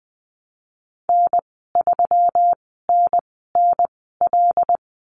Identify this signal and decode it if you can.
Morse code, heard as a keyed tone, about 20 wpm, spaced as for 15 wpm, 700 Hz. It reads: N3NNL
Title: 140 Lesson 23g - Letter f - Call Signs 20wpm